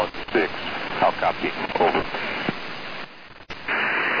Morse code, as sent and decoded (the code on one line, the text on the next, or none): none